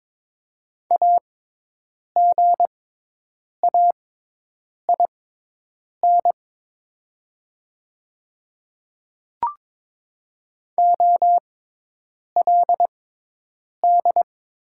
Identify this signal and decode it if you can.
Morse code, heard as a keyed tone, about 23 wpm, spaced as for 8 wpm, 700 Hz. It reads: AGAIN OLD
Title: 6x Character Spacing - Top 300 Words - 22wpm